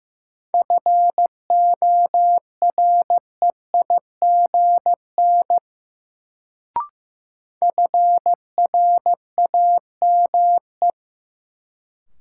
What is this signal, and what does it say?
Morse code, heard as a keyed tone, about 15 wpm, 700 Hz. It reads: FOREIGN FRAME
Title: Sentences from Top 1400 Words - New Word Drill Alt - 15wpm